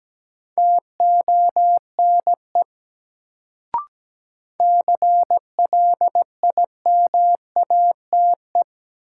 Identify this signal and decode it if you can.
Morse code, heard as a keyed tone, about 17 wpm, 700 Hz. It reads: TONE CLIMATE
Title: Sentences from Top 1600 Words - New Word Drill - 17wpm